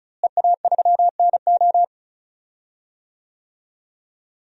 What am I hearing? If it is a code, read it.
Morse code, heard as a keyed tone, about 34 wpm, 700 Hz. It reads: EA3NO
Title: DXCC-100 Call Signs - 5 Characters - Repeated 2x - 35wpm